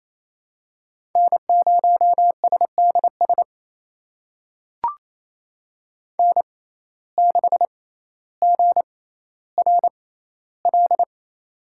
Morse code, heard as a keyed tone, about 28 wpm, spaced as for 14 wpm, 700 Hz. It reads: N0SDS N6GRL
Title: USA Call Signs - 6x Character Spacing - 28wpm